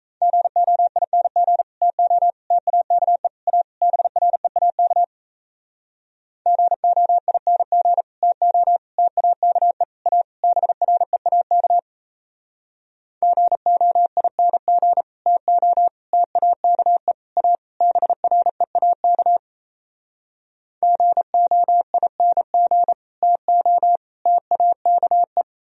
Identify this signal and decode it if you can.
Morse code, heard as a keyed tone, about 37 wpm, 700 Hz. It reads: GOING TO TAKE A BREAK GOING TO TAKE A BREAK GOING TO TAKE A BREAK GOING TO TAKE